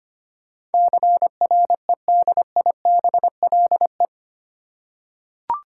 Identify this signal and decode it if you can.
Morse code, heard as a keyed tone, about 25 wpm, 700 Hz. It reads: CREDIBLE
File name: Eight Letter Words - 25wpm